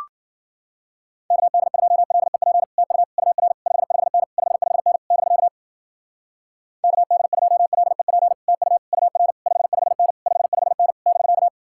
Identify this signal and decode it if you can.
Morse code, heard as a keyed tone, about 50 wpm, 700 Hz. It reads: KD2LEP TU UR 55N 55N <BK> KD2LEP TU UR 55N 55N <BK>